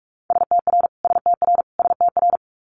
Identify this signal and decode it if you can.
Morse code, heard as a keyed tone, about 46 wpm, 700 Hz. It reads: STR STR STR
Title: Multi-letter Phonemes Repeated 3x 45wpm